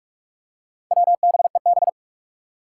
Morse code, heard as a keyed tone, about 41 wpm, 700 Hz. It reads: WBEB
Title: Four Challenging Letter-Number Combinations 45wpm